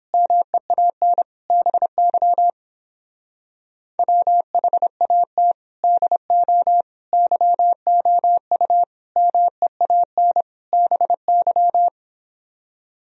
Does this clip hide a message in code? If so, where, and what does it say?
Yes: Morse code from 0.1 to 11.9 s, heard as a keyed tone, about 29 wpm, 700 Hz. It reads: MEAN BY WHAT DO YOU MEAN BY